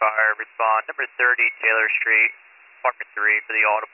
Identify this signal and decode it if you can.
Emergency Alert System (SAME) tones, heard as harsh buzzing bursts, with speech. It is heard from 0.9 to 3.6 s.